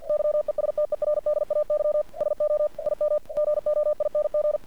Morse code, heard as a keyed tone, about 48 wpm, 600 Hz. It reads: YESTERDAY SO IM GOING
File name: archived / CRSnet